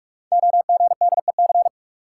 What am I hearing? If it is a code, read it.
Morse code, heard as a keyed tone, about 44 wpm, 700 Hz. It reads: OGDEC
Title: Code Groups - 5 Letters - 45wpm